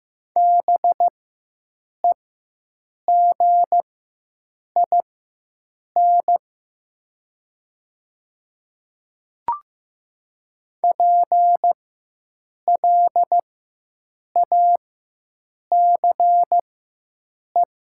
Morse code, heard as a keyed tone, about 15 wpm, spaced as for 7 wpm, 700 Hz. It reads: BEGIN PLACE
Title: 4x Character Spacing - Top 500 Words - 15wpm